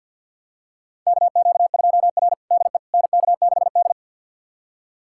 Morse code, heard as a keyed tone, about 50 wpm, 700 Hz. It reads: KQ2R DE NK6D
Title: USA Callsign DE Callsign - 4 Characters - 50wpm